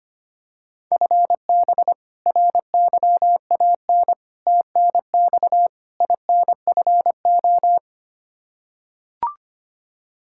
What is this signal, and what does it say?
Morse code, heard as a keyed tone, about 25 wpm, 700 Hz. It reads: FB RYAN TNX INFO
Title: K1USN SST QSO Practice 25wpm